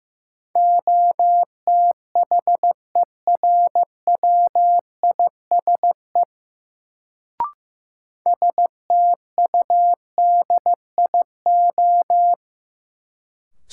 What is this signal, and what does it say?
Morse code, heard as a keyed tone, about 15 wpm, 700 Hz. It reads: OTHERWISE STUDIO
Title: Sentences from Top 1200 Words - New Word Drill Alt - 15wpm